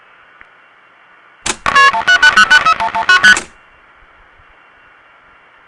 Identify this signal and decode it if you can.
Multi-frequency trunk signalling, heard as fast pairs of tones, tones of about 90 ms. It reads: KP1011ST